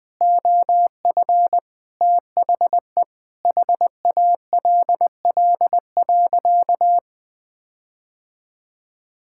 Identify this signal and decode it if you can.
Morse code, heard as a keyed tone, about 20 wpm, 700 Hz. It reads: OF THE HALL.
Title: Sentences with 8-Word Phrases - Spoken Beginning - 20wpm